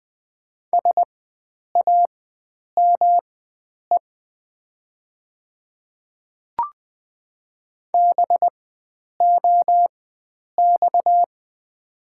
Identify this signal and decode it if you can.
Morse code, heard as a keyed tone, about 20 wpm, spaced as for 9 wpm, 700 Hz. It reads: SAME BOX
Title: 4x Character Spacing - Top 500 Words - 20wpm